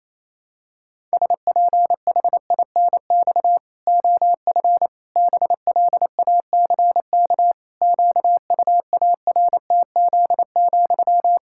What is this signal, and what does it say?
Morse code, heard as a keyed tone, about 28 wpm, 700 Hz. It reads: SPHINX OF BLACK QUARTZ,
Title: Pangram Challenge v2 - 28wpm